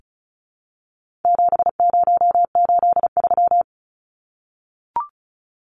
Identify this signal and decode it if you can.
Morse code, heard as a keyed tone, about 34 wpm, 700 Hz. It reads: 7083
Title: Numbers - 4-Digit 35wpm